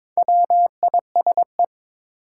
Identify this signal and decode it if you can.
Morse code, heard as a keyed tone, about 22 wpm, 700 Hz. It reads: WISE
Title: Four Letter Words - 22wpm